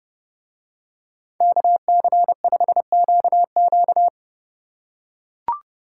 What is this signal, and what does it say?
Morse code, heard as a keyed tone, about 30 wpm, 700 Hz. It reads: KC5QQ